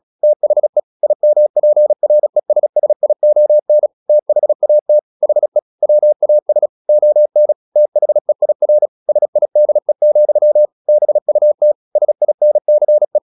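Morse code, heard as a keyed tone, about 36 wpm, 600 Hz. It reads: THE IMPRESSION THAT HE WAS ON THEIR SIDE, BUT SINCE
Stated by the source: Morse practice at 36 wpm, text of Ken Follett's novel Never, from CW med Gnister.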